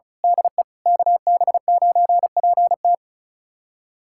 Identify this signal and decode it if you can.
Morse code, heard as a keyed tone, about 36 wpm, 700 Hz. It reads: DE KB9PT
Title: USA Callsign DE Callsign - 5 Characters - 35wpm